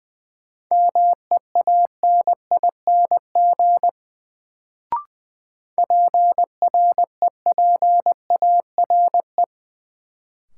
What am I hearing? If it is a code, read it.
Morse code, heard as a keyed tone, about 20 wpm, 700 Hz. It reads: MEANING PREPARE
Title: Sentences from Top 900 Words - New Word Drill - 20wpm